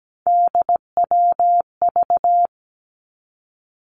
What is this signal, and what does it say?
Morse code, heard as a keyed tone, about 17 wpm, 700 Hz. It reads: DWV
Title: Three Challenging Letter-Number Combinations 17wpm